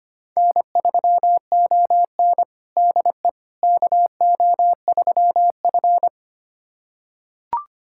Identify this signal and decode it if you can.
Morse code, heard as a keyed tone, about 25 wpm, 700 Hz. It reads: N3ON DE KO3F
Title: USA Callsign DE Callsign - 4 Characters - 25wpm